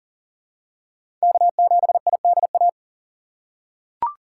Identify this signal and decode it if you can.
Morse code, heard as a keyed tone, about 40 wpm, 700 Hz. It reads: K7IDA
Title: USA Call Signs - 5 Characters - 3x Character Spacing - 40wpm